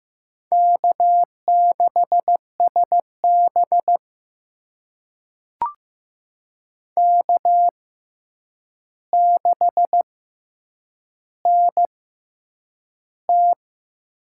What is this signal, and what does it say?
Morse code, heard as a keyed tone, about 15 wpm, spaced as for 8 wpm, 700 Hz. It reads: K6SB K6NT
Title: USA Call Signs - 6x Character Spacing - 15wpm